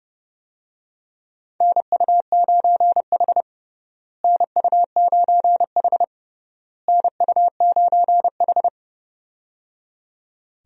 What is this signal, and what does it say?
Morse code, heard as a keyed tone, about 30 wpm, 700 Hz. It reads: NU9H NU9H NU9H
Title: USA Call Signs - 4 Characters - Repeated 3x - 30wpm